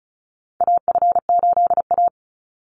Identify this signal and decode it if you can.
Morse code, heard as a keyed tone, about 36 wpm, 700 Hz. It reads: AF8A
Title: Call Signs - 35wpm